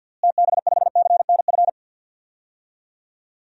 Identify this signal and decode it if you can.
Morse code, heard as a keyed tone, about 48 wpm, 700 Hz. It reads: TB5CNF
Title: DXCC-100 Call Signs - 6 Characters - Repeated 2x - 50wpm